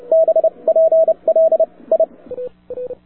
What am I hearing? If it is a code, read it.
Morse code, heard as a keyed tone, about 30 wpm, spaced as for 25 wpm, 650 Hz. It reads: BPL I
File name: wwcw08 / k5zd wwcw08 1400z